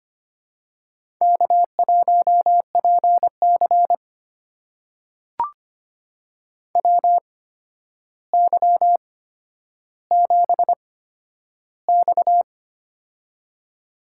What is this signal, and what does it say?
Morse code, heard as a keyed tone, about 25 wpm, spaced as for 11 wpm, 700 Hz. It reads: K1PC WY7X
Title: USA Call Signs - 4 Characters - 8x Character Spacing - 25wpm